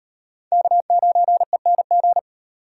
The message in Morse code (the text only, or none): K9ENG